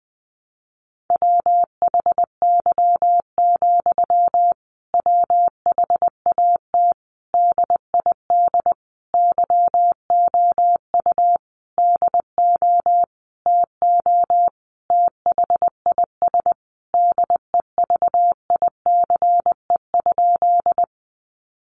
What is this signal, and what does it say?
Morse code, heard as a keyed tone, about 20 wpm, 700 Hz. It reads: WHY, WHAT DID YOU DO TO THIS DEVICE?
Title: Sentences from Top 1500 Words 20wpm